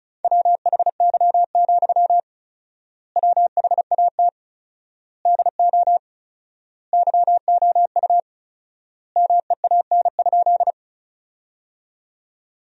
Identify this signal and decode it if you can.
Morse code, heard as a keyed tone, about 34 wpm, 700 Hz. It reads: WHY, WHAT DO YOU MEAN?
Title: 4x Word-Spacing - Sentences from Top 200 Words - 35wpm